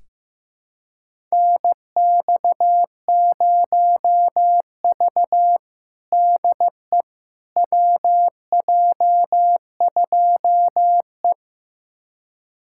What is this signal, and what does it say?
Morse code, heard as a keyed tone, about 15 wpm, 700 Hz. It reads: NX0V DE WJ2E
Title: USA Callsign DE Callsign - 4 Characters - 15wpm